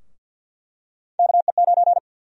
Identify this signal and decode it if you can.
Morse code, heard as a keyed tone, about 47 wpm, 700 Hz. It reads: KE9
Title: USA Call Sign Prefixes - 50wpm